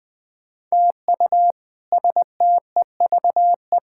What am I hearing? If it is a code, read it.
Morse code, heard as a keyed tone, about 21 wpm, 700 Hz. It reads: TU STEVE
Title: K1USN SST QSO Practice - 20wpm